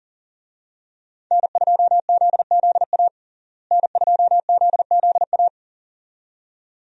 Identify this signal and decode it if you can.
Morse code, heard as a keyed tone, about 40 wpm, 700 Hz. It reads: N2ZZA N2ZZA
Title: USA Call Signs - 5 Characters - Repeated 2x - 40wpm